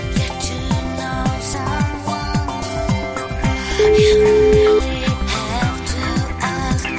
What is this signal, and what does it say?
Ringback tone (Europe, 425 Hz), with noise.